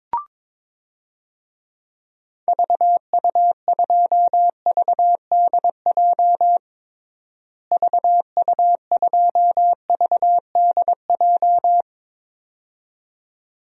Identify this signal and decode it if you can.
Morse code, heard as a keyed tone, about 22 wpm, 700 Hz. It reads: VU2VDJ VU2VDJ
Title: DXCC-100 Call Signs - 6 Characters - Repeated 2x - 22wpm